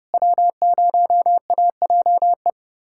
Morse code, heard as a keyed tone, about 30 wpm, 700 Hz. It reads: W0AJE
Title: USA Callsign DE Callsign - 5 Characters - 30wpm